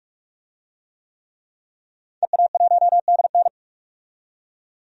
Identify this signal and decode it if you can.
Morse code, heard as a keyed tone, about 46 wpm, 700 Hz. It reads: EA1DN
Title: DXCC-100 Call Signs - 5 Characters - 45wpm